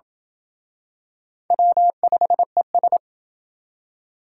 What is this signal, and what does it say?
Morse code, heard as a keyed tone, about 27 wpm, 700 Hz. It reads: W5ES